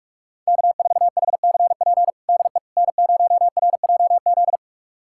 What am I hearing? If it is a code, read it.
Morse code, heard as a keyed tone, about 46 wpm, 700 Hz. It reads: K4HCP DE N0RJZ
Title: USA Callsign DE Callsign - 5 Characters - 45wpm